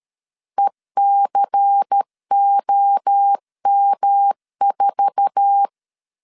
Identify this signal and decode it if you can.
Morse code, heard as a keyed tone, about 13 wpm, 800 Hz. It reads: ECOM4
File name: LEZIONI / 8 YJ45 / 8 YJ4 12wpm 2151